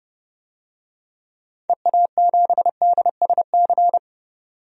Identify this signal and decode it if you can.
Morse code, heard as a keyed tone, about 30 wpm, 700 Hz. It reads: EA7DSC
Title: DXCC-100 Call Signs - 6 Characters - 30wpm